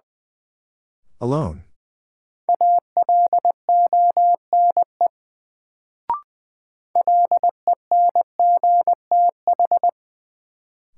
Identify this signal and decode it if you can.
Morse code, heard as a keyed tone, about 20 wpm, 700 Hz. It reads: ALONE LENGTH
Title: Sentences from Top 1000 Words - New Word Drill - 20wpm